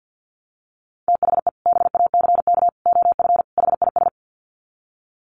Morse code, heard as a keyed tone, about 50 wpm, 700 Hz. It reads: THE BACK OF HIS
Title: Sets of 4 Words 50wpm